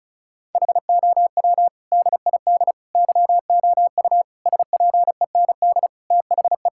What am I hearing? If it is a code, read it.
Morse code, heard as a keyed tone, about 36 wpm, 700 Hz. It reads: HOW DID YOU SPEND THE
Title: Sentences from Top 1000 Words - Review Part 5 - 35wpm